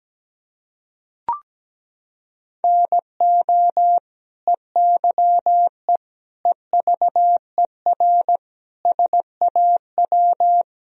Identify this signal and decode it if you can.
Morse code, heard as a keyed tone, about 17 wpm, 700 Hz. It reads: NO EYE EVER SAW
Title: Sentences from Top 1000 Words - Review Part 1 - 17wpm